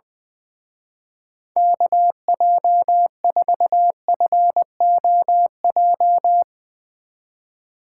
Morse code, heard as a keyed tone, about 22 wpm, 700 Hz. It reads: KJ4FOJ
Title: Speed Racing - USA 2x3 Call Signs 20wpm